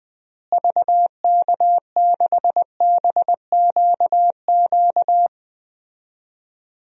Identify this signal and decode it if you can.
Morse code, heard as a keyed tone, about 20 wpm, 700 Hz. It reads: VK6BQQ
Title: DXCC-100 Call Signs - 6 Characters - Repeated 2x - 20wpm